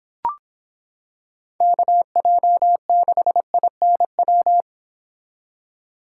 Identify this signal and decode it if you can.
Morse code, heard as a keyed tone, about 26 wpm, 700 Hz. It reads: KJ6INW